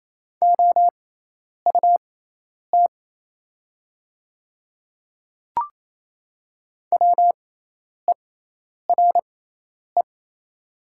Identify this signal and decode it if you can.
Morse code, heard as a keyed tone, about 28 wpm, spaced as for 10 wpm, 700 Hz. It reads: OUT WERE
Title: 6x Character Spacing - Top 100 Words - 28wpm